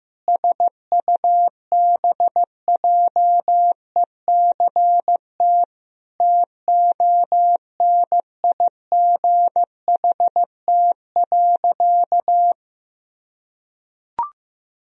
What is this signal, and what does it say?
Morse code, heard as a keyed tone, about 15 wpm, 700 Hz. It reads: SUBJECT TONIGHT.
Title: Sentences from Top 1000 Words - Review Part 4 - 15wpm